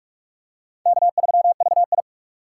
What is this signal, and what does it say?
Morse code, heard as a keyed tone, about 44 wpm, 690 Hz. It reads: K3VI
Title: USA Call Signs - 4 Characters - 3x Character Spacing - 45wpm